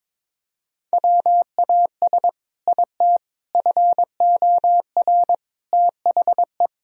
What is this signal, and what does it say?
Morse code, heard as a keyed tone, about 22 wpm, 700 Hz. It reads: WAS IT FOR THE
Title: Sentences from Top 1000 Words - Review Part 3 - 22wpm